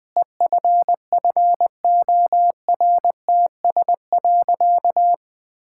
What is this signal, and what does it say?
Morse code, heard as a keyed tone, about 20 wpm, 700 Hz. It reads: EFFORTS.
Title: Difficult Sentences with Hints - 20wpm